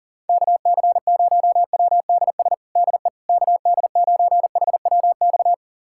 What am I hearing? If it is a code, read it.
Morse code, heard as a keyed tone, about 40 wpm, 700 Hz. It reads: KC0WDS DE KD9HWX
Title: USA Callsign DE Callsign - 6 Characters - 40wpm